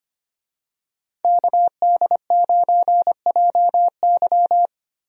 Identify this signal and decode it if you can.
Morse code, heard as a keyed tone, about 25 wpm, 700 Hz. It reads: KD9JY
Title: USA Call Signs - 5 Characters - Repeated 2x - 25wpm